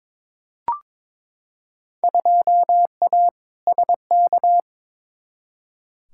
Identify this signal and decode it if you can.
Morse code, heard as a keyed tone, about 22 wpm, 700 Hz. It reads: 2A SK
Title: Field Day Exchanges Encore - 22wpm